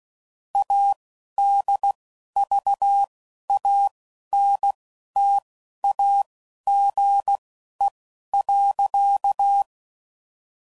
Morse code, heard as a keyed tone, about 16 wpm, spaced as for 11 wpm, 800 Hz. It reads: ADVANTAGE.
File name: CW Code Groups @10WPM / Technical Subjects / SwPwrSup